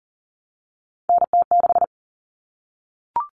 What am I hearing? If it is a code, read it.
Morse code, heard as a keyed tone, about 40 wpm, 700 Hz. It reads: NT6
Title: All Cut Numbers - 3-Digit 40wpm